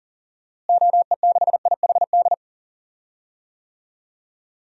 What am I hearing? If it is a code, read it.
Morse code, heard as a keyed tone, about 40 wpm, 700 Hz. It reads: OE6IHD
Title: DXCC-100 Call Signs - 6 Characters - Repeated 2x - 40wpm